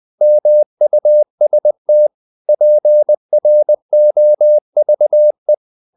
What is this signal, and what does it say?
Morse code, heard as a keyed tone, about 20 wpm, 600 Hz. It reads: MUST PROVE
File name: the old man and the sea / the old man and the sea part 1 20wpm